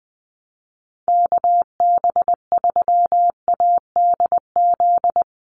Morse code, heard as a keyed tone, about 20 wpm, 700 Hz. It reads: KB3ADZ
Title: USA Call Signs 20wpm